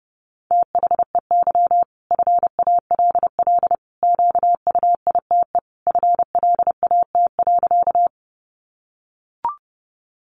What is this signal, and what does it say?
Morse code, heard as a keyed tone, about 30 wpm, 700 Hz. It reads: THEY FALL QUITE FLAT.
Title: Sentences from Top 1000 Words 30wpm